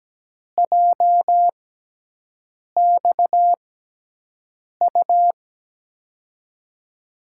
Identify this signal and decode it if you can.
Morse code, heard as a keyed tone, about 17 wpm, spaced as for 6 wpm, 700 Hz. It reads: JXU
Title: USA Call Signs - 5 Characters - 6x Character Spacing - 17wpm